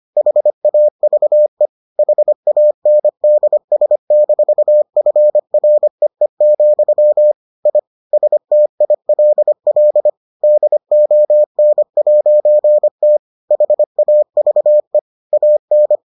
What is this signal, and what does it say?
Morse code, heard as a keyed tone, about 25 wpm, 600 Hz. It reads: HAVE HANDS-FREE, I STILL DON'T HAVE AN